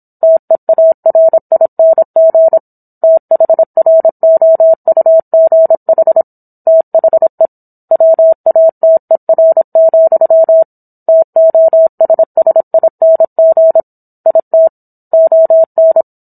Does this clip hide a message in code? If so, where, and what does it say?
Yes: Morse code from 0.2 to 16.0 s, heard as a keyed tone, about 27 wpm, 650 Hz. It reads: TEARING THROUGH THE WATER, TOSSING IT ON